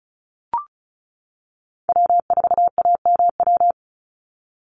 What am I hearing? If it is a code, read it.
Morse code, heard as a keyed tone, about 34 wpm, 700 Hz. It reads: W4AMW
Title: NA SOTA Chaser Callsigns 35wpm